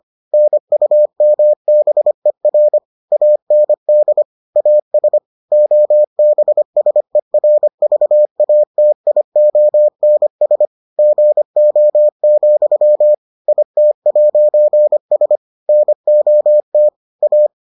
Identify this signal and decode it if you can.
Morse code, heard as a keyed tone, about 25 wpm, 600 Hz. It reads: NUMBER AND AS OBSERVATIONS GO, IT'S NOT A